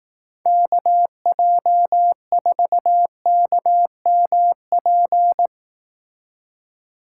Morse code, heard as a keyed tone, about 18 wpm, 700 Hz. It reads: KJ4KMP
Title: Speed Racing - USA 2x3 Call Signs 15wpm